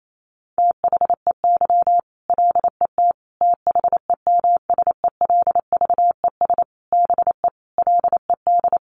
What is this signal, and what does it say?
Morse code, heard as a keyed tone, about 28 wpm, 700 Hz. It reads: THEY LET THEMSELVES BE LED
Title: A Sound Of Thunder 28wpm